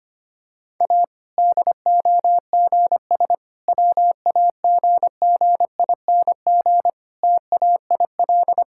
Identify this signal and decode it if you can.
Morse code, heard as a keyed tone, about 25 wpm, 700 Hz. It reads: A DOGS WAGGING TAIL
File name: practice / 25wpm / Difficult Sentences with Hints - 25wpm